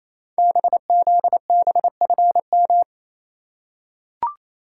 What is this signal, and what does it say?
Morse code, heard as a keyed tone, about 29 wpm, 700 Hz. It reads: BZBFM
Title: Code Groups - 5 Letters - 28wpm